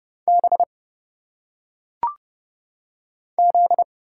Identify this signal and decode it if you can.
Morse code, heard as a keyed tone, about 31 wpm, 700 Hz. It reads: B Z
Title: Single Letters - Phonetic Pronunciation 30wpm